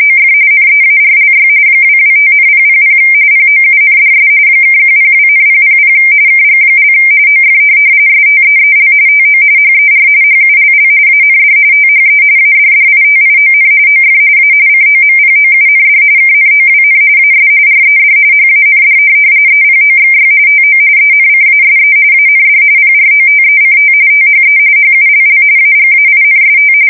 RTTY, about 45.45 baud. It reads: OF NEARLY 225,000 POUNDS (102,000 KG) OF STEEL TO HER HULL AND THE ADDITION OF A COMPOUND KNOWN AS RED HAND EPOXY TO ENCASE AND PROTECT THE HULL AGAINST
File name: fsk170 / USS-Massachusetts